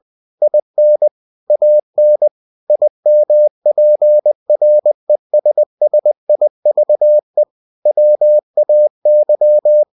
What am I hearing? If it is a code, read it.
Morse code, heard as a keyed tone, about 20 wpm, 600 Hz. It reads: IN AN IMPRESSIVE WAY
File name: cw-books / winnie-the-pooh / winnie-20wpm-page-0107